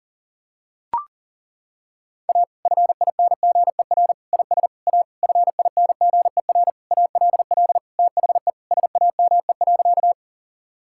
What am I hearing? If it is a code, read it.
Morse code, heard as a keyed tone, about 37 wpm, 700 Hz. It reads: A FINGER IS A FINGER ALL THE SAME.
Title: Sentences from Top 1600 Words 40wpm